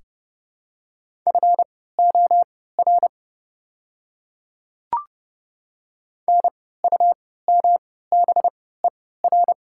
Morse code, heard as a keyed tone, about 30 wpm, spaced as for 17 wpm, 700 Hz. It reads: FOR NUMBER
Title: 3x Character Spacing - Top 100 Words - 30wpm